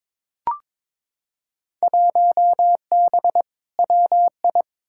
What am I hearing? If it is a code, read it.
Morse code, heard as a keyed tone, about 22 wpm, 700 Hz. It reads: 1B WI
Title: Field Day Exchanges - 22wpm